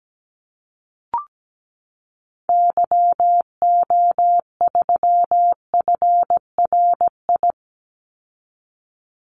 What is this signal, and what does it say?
Morse code, heard as a keyed tone, about 17 wpm, 700 Hz. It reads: YO3FRI
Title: Call Signs Repeated 3x - 17wpm